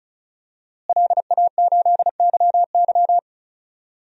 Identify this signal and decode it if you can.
Morse code, heard as a keyed tone, about 35 wpm, 700 Hz. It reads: LA8YY
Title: DXCC-100 Call Signs - 5 Characters - Repeated 3x - 35wpm